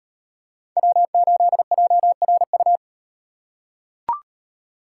Morse code, heard as a keyed tone, about 35 wpm, 700 Hz. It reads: W8JRU